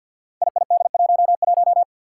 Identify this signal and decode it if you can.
Morse code, heard as a keyed tone, about 50 wpm, 700 Hz. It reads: IID11
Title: Five Challenging Letter-Number Combinations 50wpm